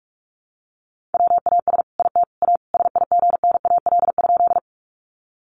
Morse code, heard as a keyed tone, about 44 wpm, 700 Hz. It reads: WAS IT A SIGNAL?
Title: Sentences from Top 1400 Words 45wpm